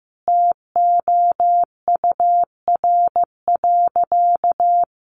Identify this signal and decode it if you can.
Morse code, heard as a keyed tone, about 15 wpm, 700 Hz. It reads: TOUR.
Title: Sentences from Top 1400 Words 15wpm